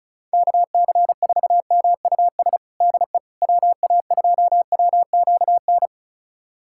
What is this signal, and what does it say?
Morse code, heard as a keyed tone, about 34 wpm, 700 Hz. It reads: KC4MUS DE WA2WQN
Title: USA Callsign DE Callsign 35wpm